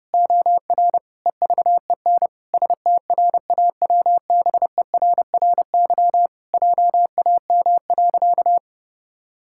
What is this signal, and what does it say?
Morse code, heard as a keyed tone, about 30 wpm, 700 Hz. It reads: OR EVEN STRAWBERRY JAM.
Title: Pangram Challenge v2 - 30wpm